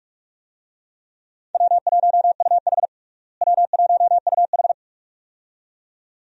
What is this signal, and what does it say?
Morse code, heard as a keyed tone, about 46 wpm, 700 Hz. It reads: W1UH W1UH
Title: USA Call Signs - 4 Characters - Repeated 2x - 45wpm